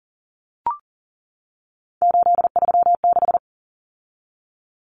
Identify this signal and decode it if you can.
Morse code, heard as a keyed tone, about 40 wpm, 700 Hz. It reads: 836
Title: Numbers - 3-Digit 40wpm